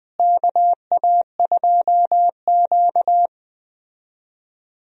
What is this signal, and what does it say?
Morse code, heard as a keyed tone, about 20 wpm, 700 Hz. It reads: KA2Q
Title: USA Call Signs - 4 Characters - Repeated 2x - 20wpm